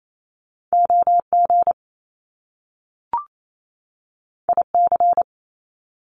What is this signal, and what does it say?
Morse code, heard as a keyed tone, about 28 wpm, 700 Hz. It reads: OG IC
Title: Top Two Letter Combinations in Words 28wpm